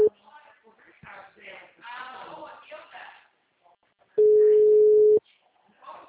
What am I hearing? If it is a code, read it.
Ringback tone (Europe, 425 Hz), with speech.